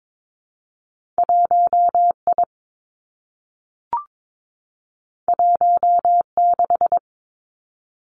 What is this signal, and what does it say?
Morse code, heard as a keyed tone, about 22 wpm, 700 Hz. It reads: 1 I 1 6
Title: Two Challenging Letter-Number Combinations 22wpm